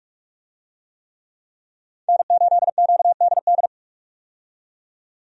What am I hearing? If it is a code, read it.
Morse code, heard as a keyed tone, about 46 wpm, 700 Hz. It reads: N8QDD
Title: USA Call Signs - 5 Characters - 45wpm